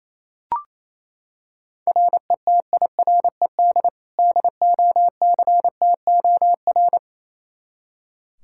Morse code, heard as a keyed tone, about 28 wpm, 700 Hz. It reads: RETIRED DOCTOR
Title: K1USN SST QSO Practice 28wpm